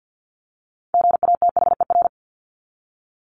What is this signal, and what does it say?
Morse code, heard as a keyed tone, about 50 wpm, 700 Hz. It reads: GATHER
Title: Top 1501-1750 Words 50wpm